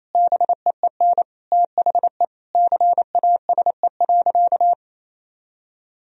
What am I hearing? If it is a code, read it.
Morse code, heard as a keyed tone, about 28 wpm, 700 Hz. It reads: BEEN THE CASE.
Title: Sentences from Top 1000 Words - Review Part 3 - 28wpm